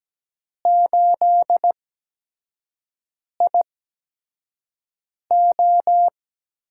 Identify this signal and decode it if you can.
Morse code, heard as a keyed tone, about 17 wpm, spaced as for 5 wpm, 700 Hz. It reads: 8IO
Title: USA Call Signs - 4 Characters - 8x Character Spacing - 17wpm